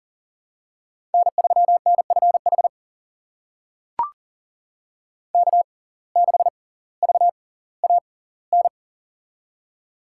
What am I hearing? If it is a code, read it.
Morse code, heard as a keyed tone, about 40 wpm, spaced as for 20 wpm, 700 Hz. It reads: N3NFH K6VAN
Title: USA Call Signs - 5 Characters - 6x Character Spacing - 40wpm